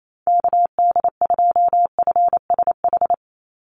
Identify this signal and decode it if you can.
Morse code, heard as a keyed tone, about 28 wpm, 690 Hz. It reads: KD2FSH
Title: Call Signs Repeated 3x - 28wpm